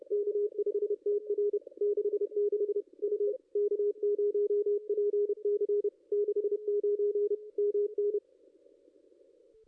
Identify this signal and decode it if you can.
Morse code, heard as a keyed tone, about 30 wpm, 440 Hz. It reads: K5TR 66 U K0PC 69 MN
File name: contests / ss-cw-k5tr-op-n5rz -2200